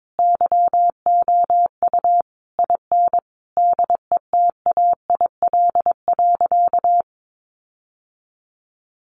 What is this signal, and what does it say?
Morse code, heard as a keyed tone, about 22 wpm, 700 Hz. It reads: YOU IN DETAIL.
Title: Sentences from Top 1500 Words 22wpm